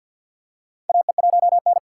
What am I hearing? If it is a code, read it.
Morse code, heard as a keyed tone, about 50 wpm, 700 Hz. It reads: AE1N
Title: K1USN SST QSO Practice - 50wpm